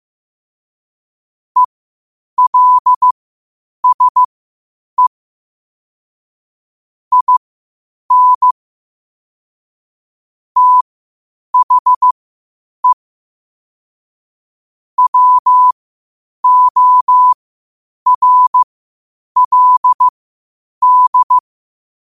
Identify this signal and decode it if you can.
Morse code, heard as a keyed tone, about 15 wpm, spaced as for 8 wpm, 1000 Hz. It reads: ELSE IN THE WORLD
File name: qotdfiles / 8wpm